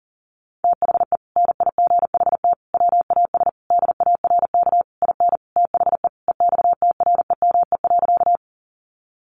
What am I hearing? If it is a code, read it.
Morse code, heard as a keyed tone, about 40 wpm, 700 Hz. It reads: THE NIGHT WAS DARK IN THE EXTREME.
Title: Sentences from Top 1300 Words 40wpm